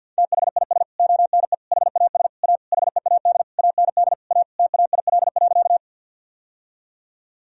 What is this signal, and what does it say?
Morse code, heard as a keyed tone, about 49 wpm, 700 Hz. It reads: THIS ONE HAS A HEAD AND A TAIL.
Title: Sentences from Top 1000 Words - Review Part 3 - 50wpm